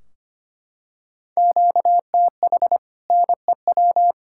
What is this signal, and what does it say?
Morse code, heard as a keyed tone, about 25 wpm, 700 Hz. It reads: QTH NEW
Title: Practice QSO - Exchange 1 - Beginning - 25wpm